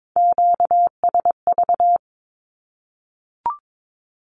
Morse code, heard as a keyed tone, about 22 wpm, 700 Hz. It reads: QSV
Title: Q-Codes Repeated 3x 22wpm